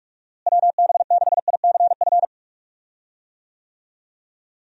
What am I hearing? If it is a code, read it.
Morse code, heard as a keyed tone, about 41 wpm, 700 Hz. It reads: WB6ICF
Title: USA Call Signs - 6 Characters - Repeated 2x - 45wpm